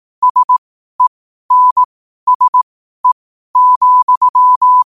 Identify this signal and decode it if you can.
Morse code, heard as a keyed tone, about 18 wpm, spaced as for 12 wpm, 1000 Hz. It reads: SENSE,
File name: qotdfiles / 12wpm